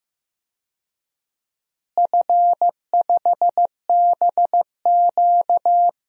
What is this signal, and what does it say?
Morse code, heard as a keyed tone, about 15 wpm, 700 Hz. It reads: F5BQ
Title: DXCC-100 Call Signs - 4 Characters - Repeated 2x - 15wpm